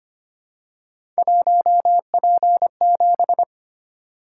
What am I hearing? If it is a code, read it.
Morse code, heard as a keyed tone, about 25 wpm, 700 Hz. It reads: 1P7
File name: Three Letter-Number Combinations - 25wpm